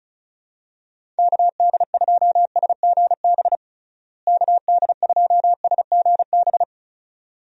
Morse code, heard as a keyed tone, about 34 wpm, 700 Hz. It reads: KD2SGB KD2SGB
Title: USA Call Signs - 6 Characters - Repeated 2x - 35wpm